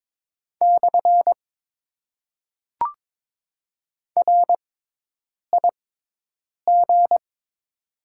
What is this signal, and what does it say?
Morse code, heard as a keyed tone, about 22 wpm, spaced as for 8 wpm, 700 Hz. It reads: / RIG
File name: Top 90 QSO Elements - 6x Character Spacing - 22wpm